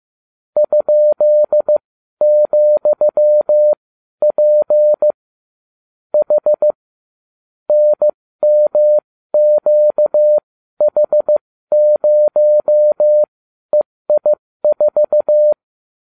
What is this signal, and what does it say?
Morse code, heard as a keyed tone, about 15 wpm, spaced as for 12 wpm, 600 Hz. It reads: ?,P H NMQH0EI4